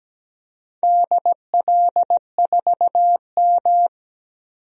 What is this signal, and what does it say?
Morse code, heard as a keyed tone, about 17 wpm, 700 Hz. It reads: DL4M
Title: DXCC-100 Call Signs - 4 Characters - 17wpm